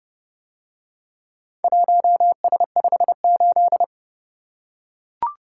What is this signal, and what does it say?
Morse code, heard as a keyed tone, about 32 wpm, 700 Hz. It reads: 1S58